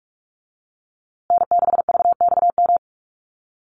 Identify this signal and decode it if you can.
Morse code, heard as a keyed tone, about 44 wpm, 700 Hz. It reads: N6VXM
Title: USA Call Signs Repeated 3x 45wpm